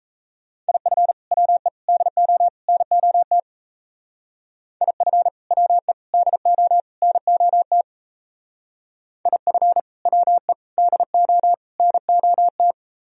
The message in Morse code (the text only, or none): IF WE DO NOT IF WE DO NOT IF WE DO NOT